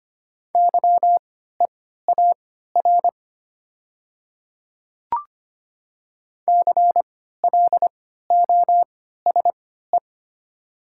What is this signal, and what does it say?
Morse code, heard as a keyed tone, about 25 wpm, spaced as for 14 wpm, 700 Hz. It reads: YEAR CLOSE
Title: 3x Character Spacing - Top 500 Words - 25wpm